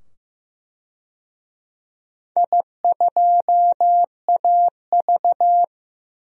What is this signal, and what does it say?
Morse code, heard as a keyed tone, about 15 wpm, 700 Hz. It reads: I2AV
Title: DXCC-100 Call Signs - 4 Characters - 15wpm